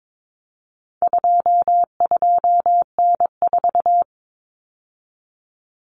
Morse code, heard as a keyed tone, about 22 wpm, 700 Hz. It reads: 22N4
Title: Common Cut Numbers - 4-Digit 22wpm